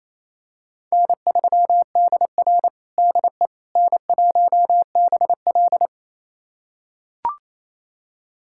Morse code, heard as a keyed tone, about 28 wpm, 700 Hz. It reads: N3DR DE N1BL
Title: USA Callsign DE Callsign - 4 Characters - 28wpm